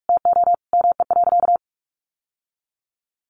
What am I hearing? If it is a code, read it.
Morse code, heard as a keyed tone, about 44 wpm, 700 Hz. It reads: TO ME.
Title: Sentences from Top 1100 Words 45wpm